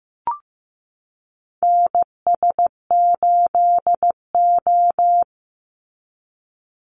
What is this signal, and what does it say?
Morse code, heard as a keyed tone, about 15 wpm, 700 Hz. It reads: NS8O
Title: Call Signs Repeated 2x - 15wpm